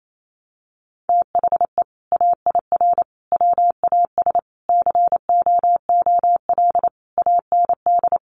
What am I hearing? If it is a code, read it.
Morse code, heard as a keyed tone, about 28 wpm, 700 Hz. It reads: THE AIR WAS COOL AND
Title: Sentences from Top 1200 Words 28wpm